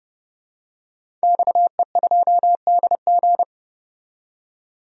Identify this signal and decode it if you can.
Morse code, heard as a keyed tone, about 30 wpm, 700 Hz. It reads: XE2DG
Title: DXCC-100 Call Signs - 5 Characters - 30wpm